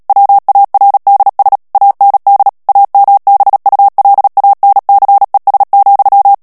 Morse code, heard as a keyed tone, about 37 wpm, 800 Hz. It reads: WARDS AND AMBULANCES,
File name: morse mp3 / ANGLAIS texte clair 2 vitesse 35 mots minute